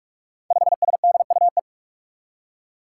Morse code, heard as a keyed tone, about 42 wpm, 700 Hz. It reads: 5SDUE
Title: Five Challenging Letter-Number Combinations 45wpm